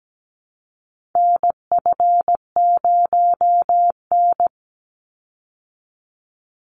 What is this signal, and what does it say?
Morse code, heard as a keyed tone, about 17 wpm, 700 Hz. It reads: NF0N
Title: Call Signs Repeated 3x - 17wpm